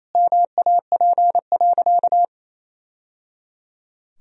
Morse code, heard as a keyed tone, about 28 wpm, 700 Hz. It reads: MAP.
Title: Sentences from Top 1000 Words - Review Part 2 - 28wpm